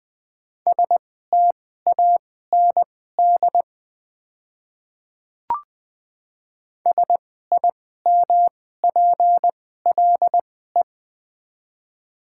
Morse code, heard as a keyed tone, about 20 wpm, spaced as for 14 wpm, 700 Hz. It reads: STAND SIMPLE
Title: 2x Character Spacing - Top 500 Words - 20wpm